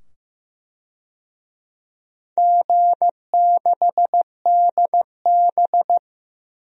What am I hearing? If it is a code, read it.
Morse code, heard as a keyed tone, about 15 wpm, 700 Hz. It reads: G6DB
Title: DXCC-100 Call Signs - 4 Characters - 15wpm